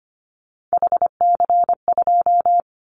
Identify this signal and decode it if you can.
Morse code, heard as a keyed tone, about 25 wpm, 700 Hz. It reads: HC2
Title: Callsign Prefixes for Top 100 DXCC - 25wpm